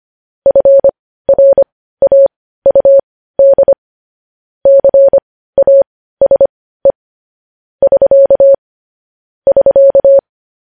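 Morse code, heard as a keyed tone, about 25 wpm, spaced as for 15 wpm, 550 Hz. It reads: FRAUD CASE <SK> <SK>
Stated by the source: Morse code news headlines sent at 15 wpm.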